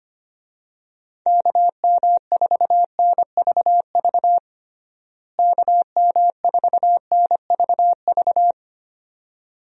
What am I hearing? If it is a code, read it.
Morse code, heard as a keyed tone, about 25 wpm, 700 Hz. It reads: KM4NVV KM4NVV
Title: USA Call Signs - 6 Characters - Repeated 2x - 25wpm